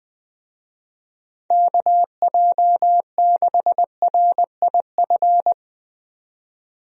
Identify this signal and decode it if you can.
Morse code, heard as a keyed tone, about 20 wpm, 700 Hz. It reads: KJ6RIF
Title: USA Call Signs - 6 Characters - 20wpm